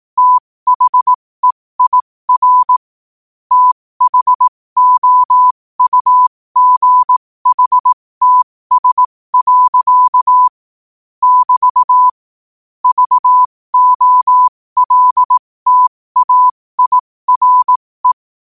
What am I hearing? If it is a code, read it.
Morse code, heard as a keyed tone, about 18 wpm, spaced as for 15 wpm, 1000 Hz. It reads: THEIR THOUGHTS. <BT> VOLTAIRE
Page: New quotes every day in morse code at 15 Words per minute.